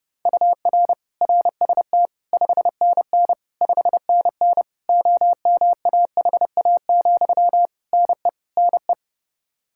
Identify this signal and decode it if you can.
Morse code, heard as a keyed tone, about 32 wpm, 700 Hz. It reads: UR RST 5NN 5NN OMAHA, NE NE
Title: Practice QSO - Exchange 1 - Beginning - 30wpm